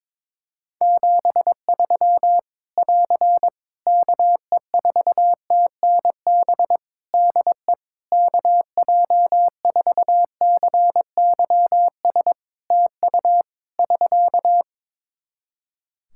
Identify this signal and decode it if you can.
Morse code, heard as a keyed tone, about 22 wpm, 700 Hz. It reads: 73 <AR> KE4TNB DE KJ4CYS TU <SK>
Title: Practice QSO - Exchange 4 - Ending - 22wpm